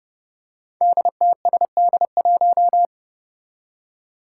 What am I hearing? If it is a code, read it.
Morse code, heard as a keyed tone, about 30 wpm, 700 Hz. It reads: DTSD1
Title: Five Challenging Letter-Number Combinations 30wpm